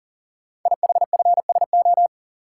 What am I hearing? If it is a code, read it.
Morse code, heard as a keyed tone, about 38 wpm, 690 Hz. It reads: IHFSO
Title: Code Groups - 5 Letters - 40wpm